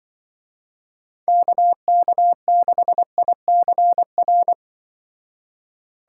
Morse code, heard as a keyed tone, about 25 wpm, 700 Hz. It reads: KK6ICR
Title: Speed Racing - USA 2x3 Call Signs 20wpm